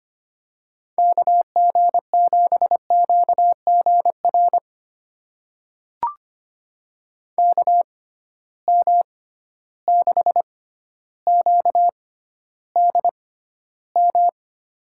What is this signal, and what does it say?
Morse code, heard as a keyed tone, about 25 wpm, spaced as for 13 wpm, 700 Hz. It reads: KG7QGR KM6QDM
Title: USA Call Signs - 6 Characters - 6x Character Spacing - 25wpm